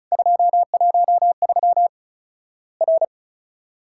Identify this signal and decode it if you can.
Morse code, heard as a keyed tone, about 34 wpm, 700 Hz. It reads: 213 R
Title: Worked All Europe DX Contest - QTC Exchange - 35wpm